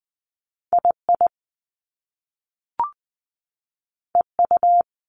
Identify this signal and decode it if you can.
Morse code, heard as a keyed tone, about 20 wpm, 700 Hz. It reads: II EU
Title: Top Two Letter Combinations in Words 20wpm